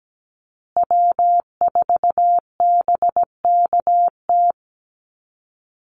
Morse code, heard as a keyed tone, about 17 wpm, 700 Hz. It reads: W4BKT
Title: USA Call Signs Repeated 2x 17wpm